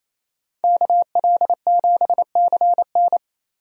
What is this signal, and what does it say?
Morse code, heard as a keyed tone, about 28 wpm, 700 Hz. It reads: KL7CN
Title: USA Call Signs - 5 Characters - Repeated 2x - 28wpm